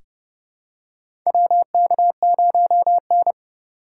Morse code, heard as a keyed tone, about 30 wpm, 700 Hz. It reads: WK0N
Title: USA Call Signs - 4 Characters - 3x Character Spacing - 30wpm